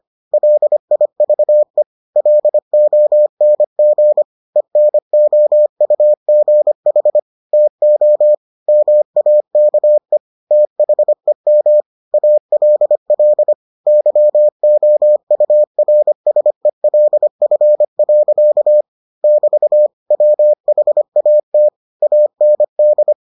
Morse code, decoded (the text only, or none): LIVE LONG ENOUGH TO MAKE THEM ALL YOURSELF. <BT> WHAT AND